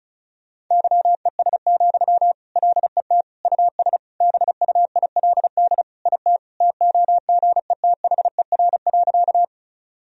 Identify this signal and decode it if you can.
Morse code, heard as a keyed tone, about 34 wpm, 700 Hz. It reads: YES, LET US BUILD IT TOGETHER.
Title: Sentences from Top 1000 Words - Review Part 1 - 35wpm